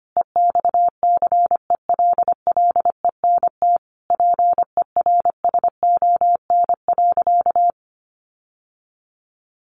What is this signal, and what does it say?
Morse code, heard as a keyed tone, about 25 wpm, 700 Hz. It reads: EXCELLENT PERSON.
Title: Sentences from Top 1000 Words 25wpm